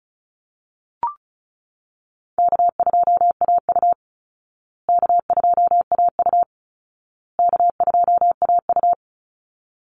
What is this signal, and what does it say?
Morse code, heard as a keyed tone, about 34 wpm, 700 Hz. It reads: K2AU K2AU K2AU
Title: Easy Call Signs Repeated 3x - 35wpm